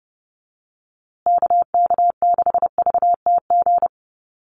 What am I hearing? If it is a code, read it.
Morse code, heard as a keyed tone, about 30 wpm, 700 Hz. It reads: KK6VTG
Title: USA Call Signs Repeated 2x 30wpm